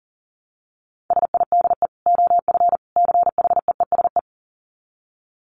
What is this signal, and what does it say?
Morse code, heard as a keyed tone, about 38 wpm, 700 Hz. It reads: SIDE OF CHEESE
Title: Dit Challenge 40wpm